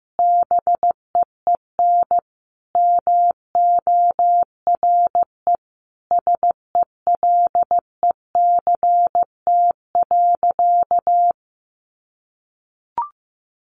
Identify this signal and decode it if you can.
Morse code, heard as a keyed tone, about 14 wpm, 700 Hz. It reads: BEEN MORE SELECT.
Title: Sentences from Top 1300 Words 15wpm